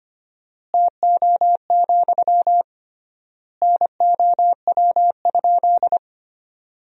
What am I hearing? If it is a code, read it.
Morse code, heard as a keyed tone, about 25 wpm, 700 Hz. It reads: TO, NOW?
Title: 3x Word-Spacing - Sentences from Top 200 Words - 25wpm